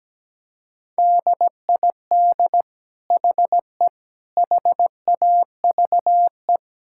Morse code, heard as a keyed tone, about 17 wpm, 700 Hz. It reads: DID HE HAVE
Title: Sentences from Top 1000 Words - Review Part 5 - 17wpm